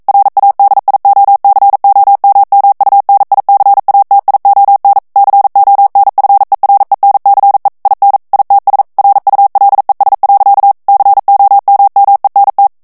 Morse code, heard as a keyed tone, about 42 wpm, 800 Hz. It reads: RADIOCOMMUNICATION CONFERENCE IN ITS RULES. COMMENT